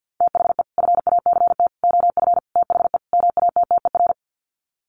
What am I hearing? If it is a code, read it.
Morse code, heard as a keyed tone, about 50 wpm, 700 Hz. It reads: THE FACT OF THE MATTER
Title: Sets of 5 Words 50wpm